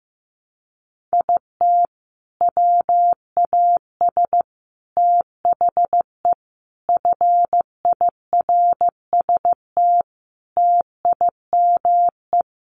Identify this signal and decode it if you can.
Morse code, heard as a keyed tone, about 15 wpm, 700 Hz. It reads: IT WAS THE FIRST TIME
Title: Sets of 5 Words 15wpm